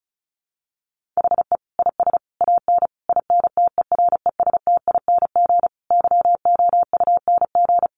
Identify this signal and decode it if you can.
Morse code, heard as a keyed tone, about 34 wpm, 700 Hz. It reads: HE IS AN INTERESTING YOUNG
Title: Sentences from Top 1000 Words 35wpm